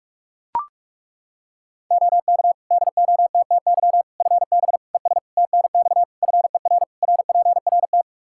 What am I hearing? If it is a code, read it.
Morse code, heard as a keyed tone, about 41 wpm, 700 Hz. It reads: OK DOTTY FB ES TNX FER RPRT
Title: K1USN SST QSO Practice 45wpm